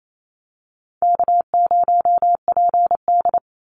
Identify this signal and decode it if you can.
Morse code, heard as a keyed tone, about 28 wpm, 700 Hz. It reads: K0PD
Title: Call Signs - 28wpm